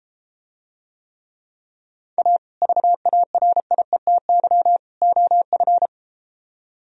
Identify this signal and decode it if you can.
Morse code, heard as a keyed tone, about 33 wpm, 700 Hz. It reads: A VARIETY OF